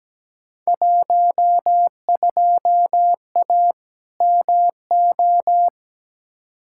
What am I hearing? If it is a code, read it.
Morse code, heard as a keyed tone, about 17 wpm, 700 Hz. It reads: 12A MO
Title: Field Day Exchanges Encore - 17wpm